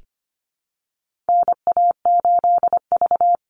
Morse code, heard as a keyed tone, about 25 wpm, 700 Hz. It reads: NA8V
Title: Call Signs - 25wpm